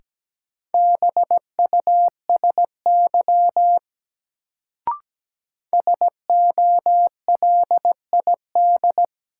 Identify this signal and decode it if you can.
Morse code, heard as a keyed tone, about 17 wpm, 700 Hz. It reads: BUSY SOLID
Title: Sentences from Top 1500 Words - New Word Drill Alt - 17wpm